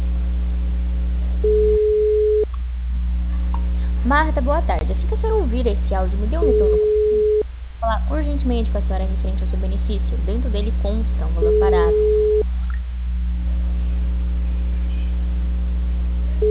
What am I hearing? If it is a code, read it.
Ringback tone (Europe, 425 Hz), with speech.